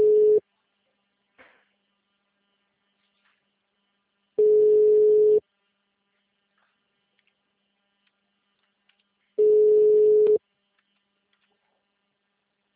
Ringback tone (Europe, 425 Hz).